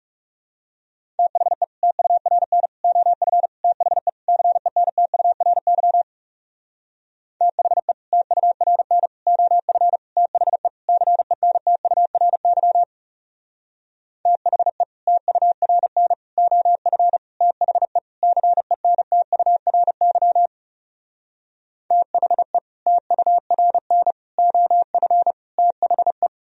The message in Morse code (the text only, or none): THE TURN OF THE CENTURY THE TURN OF THE CENTURY THE TURN OF THE CENTURY THE TURN OF THE